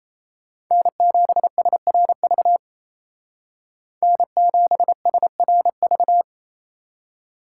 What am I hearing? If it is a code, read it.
Morse code, heard as a keyed tone, about 35 wpm, 700 Hz. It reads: N7SRV N7SRV